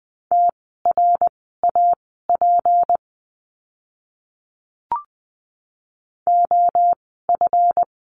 Morse code, heard as a keyed tone, about 20 wpm, spaced as for 15 wpm, 700 Hz. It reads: TRAP OF
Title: 153 Lesson 25f - Letter p - Words 20wpm